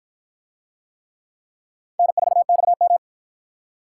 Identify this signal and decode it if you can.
Morse code, heard as a keyed tone, about 46 wpm, 700 Hz. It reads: N4XM